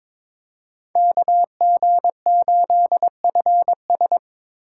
Morse code, heard as a keyed tone, about 26 wpm, 700 Hz. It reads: KG8FS
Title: Speed Racing - USA 2x2 Call Signs 22wpm